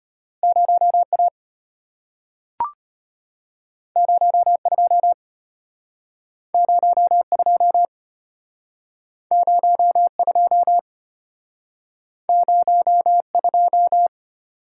Morse code, heard as a keyed tone, about 34 wpm, 700 Hz. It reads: 0A 02 02 02 02